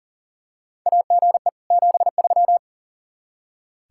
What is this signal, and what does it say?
Morse code, heard as a keyed tone, about 40 wpm, 700 Hz. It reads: AGE 73
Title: Practice QSO - Exchange 3 - More Info - 40wpm